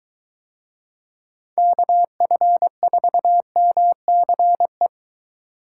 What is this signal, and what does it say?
Morse code, heard as a keyed tone, about 23 wpm, 700 Hz. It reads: KF4MCE